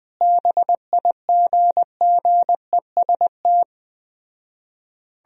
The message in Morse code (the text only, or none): BIGGEST